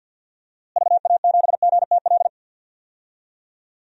Morse code, heard as a keyed tone, about 53 wpm, 700 Hz. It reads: VA7GTL
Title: DXCC-100 Call Signs - 6 Characters - Repeated 3x - 50wpm